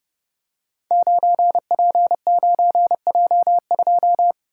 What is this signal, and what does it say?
Morse code, heard as a keyed tone, about 30 wpm, 700 Hz. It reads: 9P9J2
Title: Code Groups - 5 Letter:Numbers - 30wpm